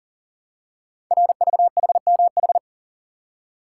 Morse code, heard as a keyed tone, about 39 wpm, 690 Hz. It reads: RVHMH
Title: Code Groups - 5 Letters - 40wpm